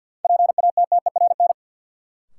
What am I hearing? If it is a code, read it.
Morse code, heard as a keyed tone, about 53 wpm, 700 Hz. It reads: PATTERN
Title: Sentences from Top 1000 Words - New Word Drill - 50wpm